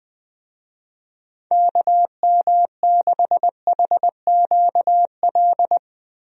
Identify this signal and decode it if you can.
Morse code, heard as a keyed tone, about 21 wpm, 700 Hz. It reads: KM6HQL
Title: USA Call Signs - 6 Characters - Repeated 3x - 20wpm